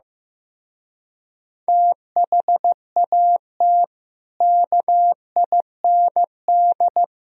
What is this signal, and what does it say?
Morse code, heard as a keyed tone, about 15 wpm, 700 Hz. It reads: THAT KIND
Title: Speed-Racing - Sets of 3 Words 15wpm